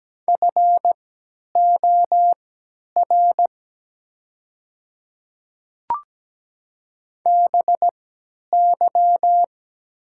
Morse code, heard as a keyed tone, about 17 wpm, spaced as for 10 wpm, 700 Hz. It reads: FOR BY
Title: 3x Character Spacing - Top 100 Words - 17wpm